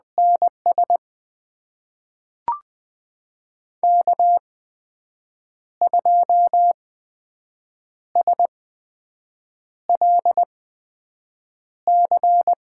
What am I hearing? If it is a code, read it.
Morse code, heard as a keyed tone, about 20 wpm, spaced as for 6 wpm, 700 Hz. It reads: NS K2SLC
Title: USA Call Signs - 5 Characters - 8x Character Spacing - 20wpm